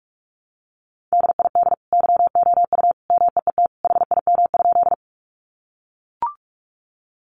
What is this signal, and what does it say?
Morse code, heard as a keyed tone, about 44 wpm, 700 Hz. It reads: DID YOU MEET HIM?
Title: Sentences from Top 1100 Words 45wpm